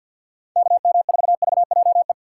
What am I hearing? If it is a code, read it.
Morse code, heard as a keyed tone, about 47 wpm, 700 Hz. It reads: KM4VJE
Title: USA Call Signs - 6 Characters - 3x Character Spacing - 50wpm